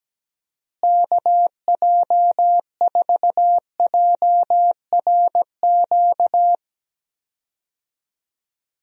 Morse code, heard as a keyed tone, about 17 wpm, 700 Hz. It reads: KJ4JRQ
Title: USA Call Signs - 6 Characters - Repeated 2x - 17wpm